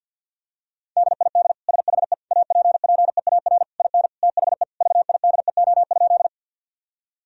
Morse code, heard as a keyed tone, about 50 wpm, 700 Hz. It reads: DID SHE APPEAR IN THE VIDEO?
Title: Sentences from Top 1000 Words - Review Part 4 - 50wpm